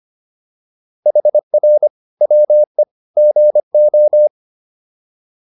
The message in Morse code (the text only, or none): HR WE GO